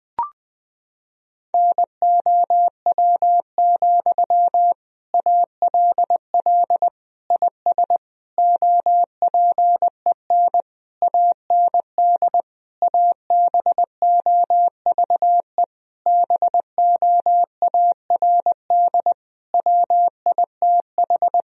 Morse code, heard as a keyed tone, about 21 wpm, 700 Hz. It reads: NOW, ALL IS OPEN AND ABOVE BOARD WITH
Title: Sentences from Top 1000 Words - Review Part 4 - 20wpm